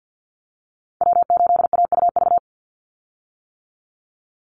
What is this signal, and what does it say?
Morse code, heard as a keyed tone, about 50 wpm, 700 Hz. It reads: W8AUV
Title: USA Call Signs 50wpm